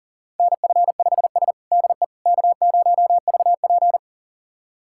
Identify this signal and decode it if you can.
Morse code, heard as a keyed tone, about 40 wpm, 700 Hz. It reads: NF5S DE K0VP
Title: USA Callsign DE Callsign - 4 Characters - 40wpm